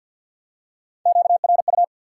Morse code, heard as a keyed tone, about 53 wpm, 700 Hz. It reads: QRU
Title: Top 180 QSO Elements 50wpm